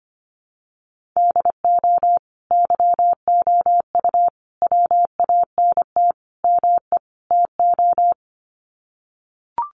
Morse code, heard as a keyed tone, about 25 wpm, 700 Hz. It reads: DO YOU WANT ME TO
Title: Sets of 5 Words 25wpm